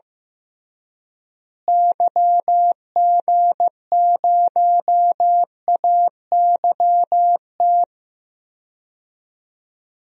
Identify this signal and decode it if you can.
Morse code, heard as a keyed tone, about 15 wpm, 700 Hz. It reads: YG0AYT
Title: DXCC-100 Call Signs - 6 Characters - Repeated 3x - 15wpm